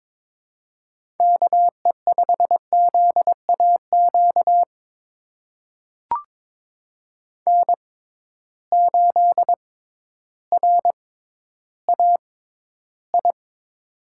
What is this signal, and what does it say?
Morse code, heard as a keyed tone, about 22 wpm, spaced as for 12 wpm, 700 Hz. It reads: KE5ZAQ N8RAI
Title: USA Call Signs - 6x Character Spacing - 22wpm